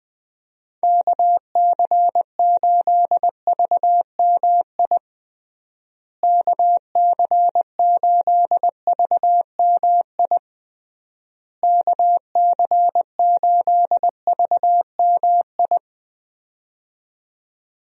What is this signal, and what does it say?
Morse code, heard as a keyed tone, about 21 wpm, 700 Hz. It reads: KC8VMI KC8VMI KC8VMI
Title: USA Call Signs - 6 Characters - Repeated 3x - 20wpm